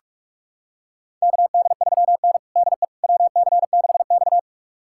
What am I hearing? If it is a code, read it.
Morse code, heard as a keyed tone, about 46 wpm, 700 Hz. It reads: KD3N DE WC6X
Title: USA Callsign DE Callsign - 4 Characters - 45wpm